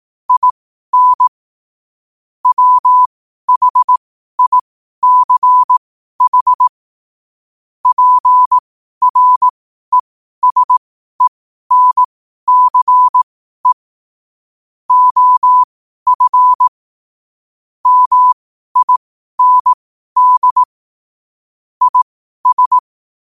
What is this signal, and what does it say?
Morse code, heard as a keyed tone, about 18 wpm, spaced as for 12 wpm, 1000 Hz. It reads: IN WHICH PRESENCE OF MIND IS
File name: qotdfiles / 12wpm